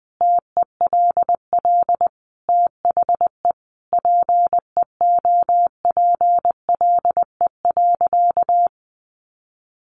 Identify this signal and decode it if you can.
Morse code, heard as a keyed tone, about 20 wpm, 700 Hz. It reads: TELL THE PEOPLE.
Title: Sentences from Top 1200 Words 20wpm